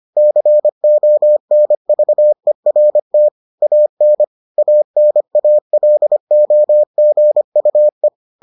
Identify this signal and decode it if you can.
Morse code, heard as a keyed tone, about 25 wpm, 600 Hz. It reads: CONVERT AN ANALOGUE